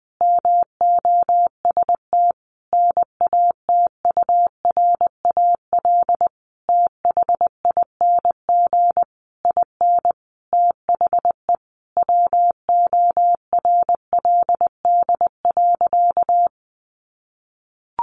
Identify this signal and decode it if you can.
Morse code, heard as a keyed tone, about 19 wpm, 700 Hz. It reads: MOST NATURAL THING IN THE WORLD.
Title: Sentences from Top 700 Words 20wpm